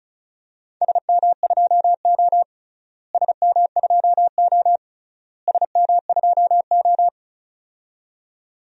Morse code, heard as a keyed tone, about 36 wpm, 700 Hz. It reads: SM2O SM2O SM2O
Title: DXCC-100 Call Signs - 4 Characters - Repeated 3x - 35wpm